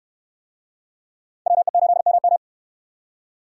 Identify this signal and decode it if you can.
Morse code, heard as a keyed tone, about 67 wpm, 700 Hz. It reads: WE8MM